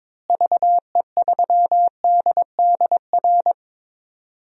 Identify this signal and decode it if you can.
Morse code, heard as a keyed tone, about 22 wpm, 700 Hz. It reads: VE3DDR
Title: DXCC-100 Call Signs - 6 Characters - Repeated 3x - 22wpm